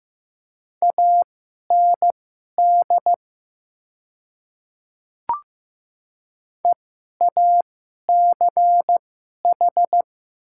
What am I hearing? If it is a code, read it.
Morse code, heard as a keyed tone, about 15 wpm, spaced as for 11 wpm, 700 Hz. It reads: AND EACH
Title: 2x Character Spacing - Top 100 Words - 15wpm